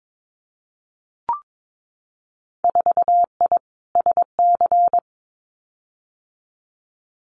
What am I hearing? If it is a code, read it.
Morse code, heard as a keyed tone, about 22 wpm, 700 Hz. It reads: 4I SC
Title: Winter Field Day Exchanges 22wpm